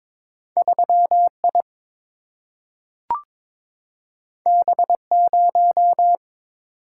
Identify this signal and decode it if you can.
Morse code, heard as a keyed tone, about 22 wpm, 700 Hz. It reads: 3I B0
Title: Single-Letter Single-Number 22wpm